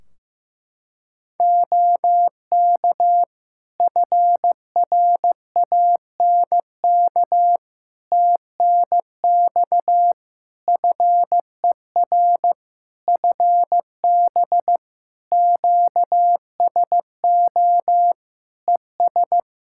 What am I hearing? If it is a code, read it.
Morse code, heard as a keyed tone, about 15 wpm, 700 Hz. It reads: OK FRANK TNX FER FB QSO ES
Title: Practice QSO - Exchange 4 - Ending - 15wpm